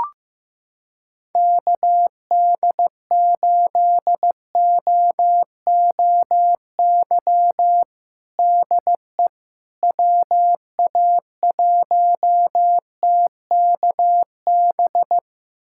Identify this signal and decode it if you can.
Morse code, heard as a keyed tone, about 15 wpm, 700 Hz. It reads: KD8OOY DE WA1TKB
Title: USA Callsign DE Callsign - 6 Characters - 15wpm